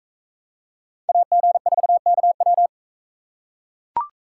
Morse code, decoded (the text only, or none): AG4KW